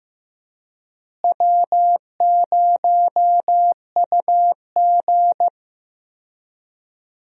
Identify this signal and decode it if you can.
Morse code, heard as a keyed tone, about 15 wpm, 700 Hz. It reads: W0UG
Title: USA Call Signs - 4 Characters - Repeated 3x - 15wpm